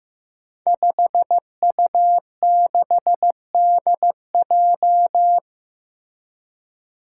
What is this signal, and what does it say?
Morse code, heard as a keyed tone, about 15 wpm, 700 Hz. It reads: 5U6DJ
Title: Five Challenging Letter-Number Combinations 15wpm